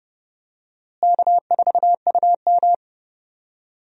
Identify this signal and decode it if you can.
Morse code, heard as a keyed tone, about 30 wpm, 700 Hz. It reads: K4UM
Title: USA Call Signs - 4 Characters - Repeated 2x - 30wpm